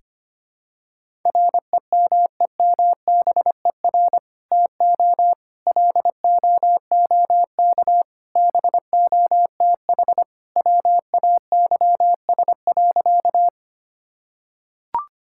Morse code, heard as a keyed tone, about 25 wpm, 700 Hz. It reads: REMEMBER TO LOOK BOTH WAYS.
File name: practice / 25wpm / Sentences with 2-Word Phrases - 25wpm